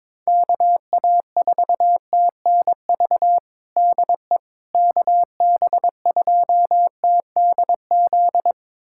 Morse code, heard as a keyed tone, about 22 wpm, 700 Hz. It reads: KA4TNV DE KB2TDZ
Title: USA Callsign DE Callsign - 6 Characters - 22wpm